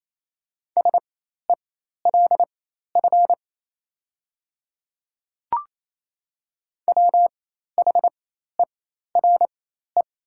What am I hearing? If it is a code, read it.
Morse code, heard as a keyed tone, about 27 wpm, spaced as for 13 wpm, 700 Hz. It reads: SELF WHERE
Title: 4x Character Spacing - Top 300 Words - 28wpm